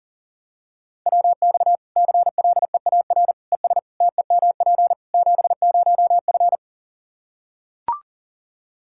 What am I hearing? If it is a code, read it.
Morse code, heard as a keyed tone, about 40 wpm, 700 Hz. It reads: WX CLEAR ES TEMP 70F
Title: K1USN SST QSO Practice 40wpm